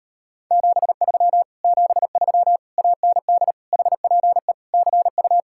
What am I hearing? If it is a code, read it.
Morse code, heard as a keyed tone, about 33 wpm, 700 Hz. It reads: 73 73 AND HPE CU